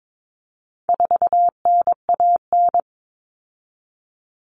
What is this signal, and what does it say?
Morse code, heard as a keyed tone, about 22 wpm, 700 Hz. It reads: 4NAN
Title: Common Cut Numbers - 4-Digit 22wpm